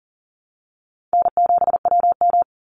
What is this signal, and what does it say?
Morse code, heard as a keyed tone, about 40 wpm, 700 Hz. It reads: N7WM
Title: NA SOTA Chaser Callsigns 40wpm